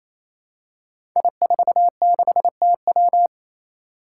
Morse code, heard as a keyed tone, about 27 wpm, 700 Hz. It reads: I46TW
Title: Five Challenging Letter-Number Combinations 28wpm